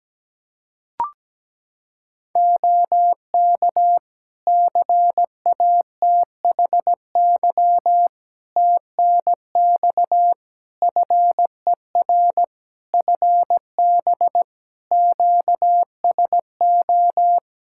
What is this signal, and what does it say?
Morse code, heard as a keyed tone, about 17 wpm, 700 Hz. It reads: OK CATHY TNX FER FB QSO
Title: Practice QSO - Exchange 4 - Ending - 17wpm